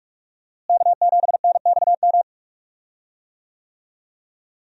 Morse code, heard as a keyed tone, about 46 wpm, 700 Hz. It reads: K7NXM
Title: USA Call Signs - 5 Characters - Repeated 2x - 45wpm